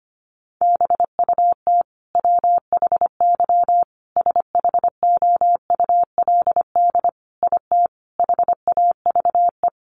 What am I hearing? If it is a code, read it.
Morse code, heard as a keyed tone, about 25 wpm, 700 Hz. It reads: BUT WHY SHOULD IT HAVE
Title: Sentences from Top 1200 Words 25wpm